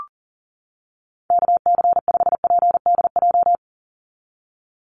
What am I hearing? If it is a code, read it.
Morse code, heard as a keyed tone, about 40 wpm, 700 Hz. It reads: KC5PDJ
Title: USA Call Signs 40wpm